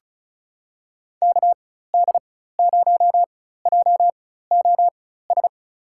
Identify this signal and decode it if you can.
Morse code, heard as a keyed tone, about 36 wpm, spaced as for 16 wpm, 700 Hz. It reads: KD0JOS
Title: USA Call Signs - 4x Character Spacing - 35wpm